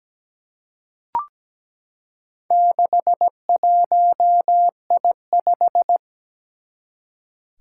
Morse code, heard as a keyed tone, about 17 wpm, 700 Hz. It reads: 61I5
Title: Four Challenging Letter-Number Combinations 17wpm